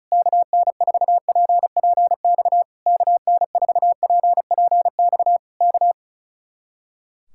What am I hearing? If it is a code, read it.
Morse code, heard as a keyed tone, about 33 wpm, 700 Hz. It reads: KN4PPX KN4PPX K
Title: Speed-Racing Common QSO Phrases - 35wpm